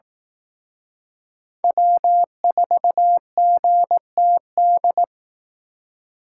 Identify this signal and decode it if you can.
Morse code, heard as a keyed tone, about 18 wpm, 700 Hz. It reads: W4GTD
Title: Speed Racing - USA 1x3 Call Signs 15wpm